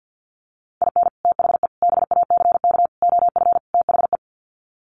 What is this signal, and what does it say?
Morse code, heard as a keyed tone, about 50 wpm, 700 Hz. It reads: IN THE BACK OF THE
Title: Sets of 5 Words 50wpm